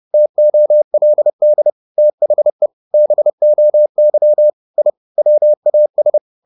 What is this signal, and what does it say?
Morse code, heard as a keyed tone, about 30 wpm, 600 Hz. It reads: TOLD THE BOY I WAS
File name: the old man and the sea / the old man and the sea 1 30wpm